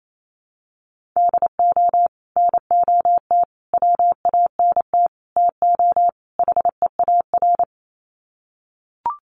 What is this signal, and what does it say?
Morse code, heard as a keyed tone, about 28 wpm, 700 Hz. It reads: DO NOT WANT TO HEAR
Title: Sets of 5 Words 28wpm